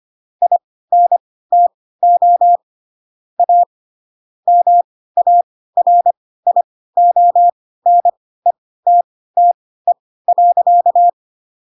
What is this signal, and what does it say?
Morse code, heard as a keyed tone, about 25 wpm, spaced as for 16 wpm, 700 Hz. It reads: INTO A MARIONETTE.
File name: Pinocchio / Pinocchio ch3 10 25 ik1hge